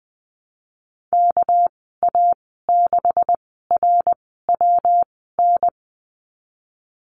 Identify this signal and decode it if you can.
Morse code, heard as a keyed tone, about 20 wpm, spaced as for 15 wpm, 700 Hz. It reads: KA6RWN
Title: 231 Lesson 37g - Letter x - Call Signs 20wpm